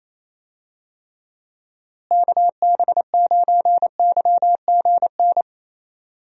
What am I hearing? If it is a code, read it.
Morse code, heard as a keyed tone, about 27 wpm, 700 Hz. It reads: KB9YGN
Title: USA Call Signs - 6 Characters - Repeated 3x - 28wpm